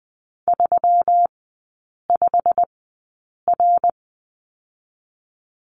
Morse code, heard as a keyed tone, about 20 wpm, spaced as for 8 wpm, 700 Hz. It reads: 35R
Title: 212 Lesson 34e - Number 8 - Three Characters 20wpm